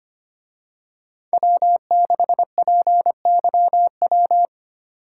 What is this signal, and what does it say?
Morse code, heard as a keyed tone, about 28 wpm, 700 Hz. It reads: W6PYW
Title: Speed Racing - USA 1x3 Call Signs 25wpm